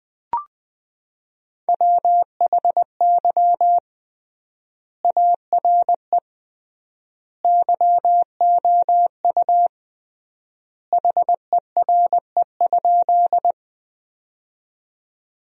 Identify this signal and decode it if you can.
Morse code, heard as a keyed tone, about 20 wpm, 700 Hz. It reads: WHY ARE YOU HERE?
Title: 3x Word-Spacing - Sentences from Top 200 Words - 20wpm